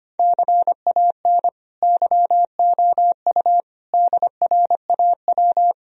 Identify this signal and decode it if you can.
Morse code, heard as a keyed tone, about 25 wpm, 700 Hz. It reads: CAN YOU DRAW
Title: Sentences from Top 1000 Words - Review Part 3 - 25wpm